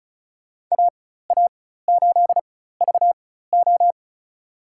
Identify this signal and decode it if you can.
Morse code, heard as a keyed tone, about 36 wpm, spaced as for 16 wpm, 700 Hz. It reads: AA8VO
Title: USA Call Signs - 5 Characters - 4x Character Spacing - 35wpm